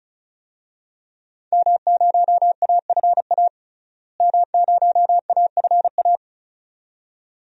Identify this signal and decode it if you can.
Morse code, heard as a keyed tone, about 34 wpm, 700 Hz. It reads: M0AFA M0AFA
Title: DXCC-100 Call Signs - 5 Characters - Repeated 2x - 35wpm